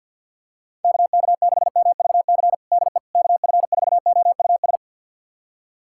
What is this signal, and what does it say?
Morse code, heard as a keyed tone, about 52 wpm, 700 Hz. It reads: KK6MVC DE KF4OUS
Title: USA Callsign DE Callsign - 6 Characters - 50wpm